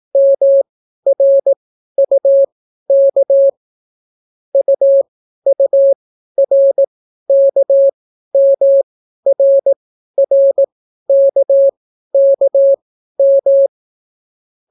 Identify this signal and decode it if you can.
Morse code, heard as a keyed tone, about 18 wpm, spaced as for 12 wpm, 550 Hz. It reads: MRUK UURKMRRKKM